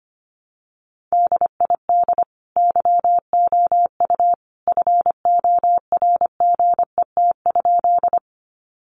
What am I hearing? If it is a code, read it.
Morse code, heard as a keyed tone, about 25 wpm, 700 Hz. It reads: DID YOU FORGET?
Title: Sentences from Top 1200 Words 25wpm